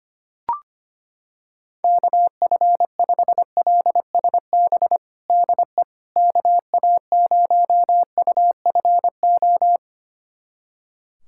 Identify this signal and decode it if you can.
Morse code, heard as a keyed tone, about 25 wpm, 700 Hz. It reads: KF5LSB DE KA0UFO
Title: USA Callsign DE Callsign 25wpm